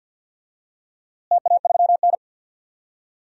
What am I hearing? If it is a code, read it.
Morse code, heard as a keyed tone, about 52 wpm, 700 Hz. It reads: TA3N
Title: DXCC-100 Call Signs - 4 Characters - Repeated 2x - 50wpm